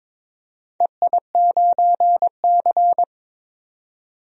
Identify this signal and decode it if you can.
Morse code, heard as a keyed tone, about 22 wpm, 700 Hz. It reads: EI9C
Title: DXCC-100 Call Signs - 4 Characters - 22wpm